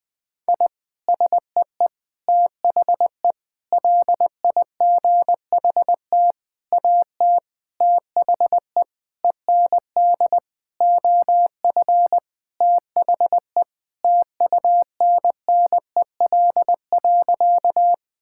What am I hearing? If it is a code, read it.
Morse code, heard as a keyed tone, about 20 wpm, 700 Hz. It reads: I SEE THE LIGHT AT THE END OF THE TUNNEL.
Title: Sentences with 8-Word Phrases - Spoken Beginning - 20wpm